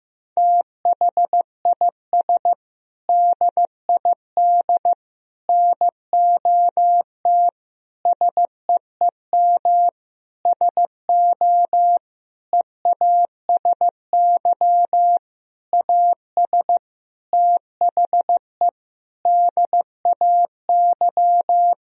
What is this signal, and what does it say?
Morse code, heard as a keyed tone, about 15 wpm, 700 Hz. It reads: THIS DID NOT SEEM SO EASY AS THE DAY
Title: Sentences from Top 1000 Words - Review Part 1 - 15wpm